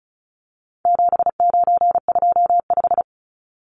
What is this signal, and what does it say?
Morse code, heard as a keyed tone, about 34 wpm, 700 Hz. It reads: 7925
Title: Numbers - 4-Digit 35wpm